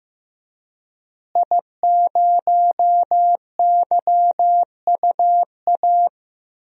Morse code, heard as a keyed tone, about 15 wpm, 700 Hz. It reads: I0YUA
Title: DXCC-100 Call Signs - 5 Characters - Repeated 3x - 15wpm